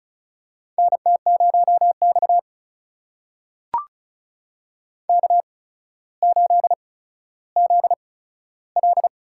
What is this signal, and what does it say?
Morse code, heard as a keyed tone, about 36 wpm, spaced as for 15 wpm, 700 Hz. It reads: NT0X K8ZL
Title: USA Call Signs - 4 Characters - 8x Character Spacing - 35wpm